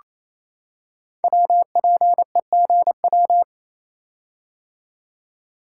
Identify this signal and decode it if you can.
Morse code, heard as a keyed tone, about 28 wpm, 700 Hz. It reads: WPEGW
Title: Code Groups - 5 Letter:Numbers - 28wpm